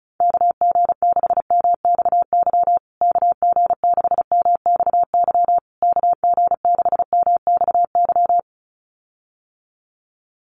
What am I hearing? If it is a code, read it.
Morse code, heard as a keyed tone, about 34 wpm, 700 Hz. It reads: KG6MXY KG6MXY KG6MXY
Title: USA Call Signs Repeated 3x 35wpm